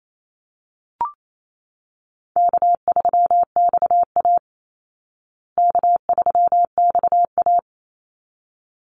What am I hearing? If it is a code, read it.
Morse code, heard as a keyed tone, about 28 wpm, 700 Hz. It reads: K3XA K3XA
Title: Easy Call Signs Repeated 2x - 28wpm